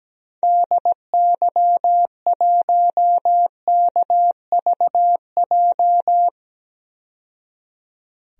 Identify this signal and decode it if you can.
Morse code, heard as a keyed tone, about 17 wpm, 700 Hz. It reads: DY1KVJ
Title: DXCC-100 Call Signs - 6 Characters - 17wpm